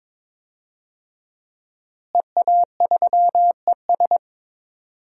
Morse code, heard as a keyed tone, about 22 wpm, 700 Hz. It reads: EA3ES
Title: DXCC-100 Call Signs - 5 Characters - Repeated 2x - 22wpm